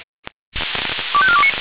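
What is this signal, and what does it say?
Five-tone selective call (ZVEI), heard as quick runs of five tones, tones of about 70 ms. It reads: 25429
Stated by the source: ZVEI five-tone selective call received by SDR.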